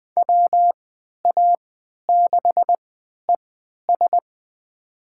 Morse code, heard as a keyed tone, about 21 wpm, spaced as for 11 wpm, 700 Hz. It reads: WA6ES
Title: USA Call Signs - 3x Character Spacing - 20wpm